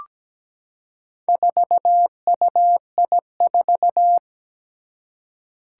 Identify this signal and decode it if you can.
Morse code, heard as a keyed tone, about 17 wpm, 700 Hz. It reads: 4UI4
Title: Four Challenging Letter-Number Combinations 17wpm